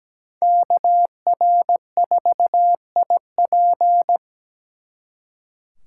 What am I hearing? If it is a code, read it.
Morse code, heard as a keyed tone, about 20 wpm, 700 Hz. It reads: KR4IP
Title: Speed Racing - USA 2x2 Call Signs 17wpm